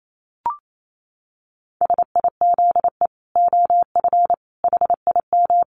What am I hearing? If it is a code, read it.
Morse code, heard as a keyed tone, about 28 wpm, 700 Hz. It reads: SIZE OF HIM
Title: Dit Challenge 28wpm